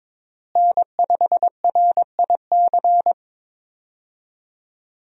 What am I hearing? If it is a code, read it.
Morse code, heard as a keyed tone, about 22 wpm, 700 Hz. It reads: N5RIC
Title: USA Call Signs - 5 Characters - Repeated 3x - 22wpm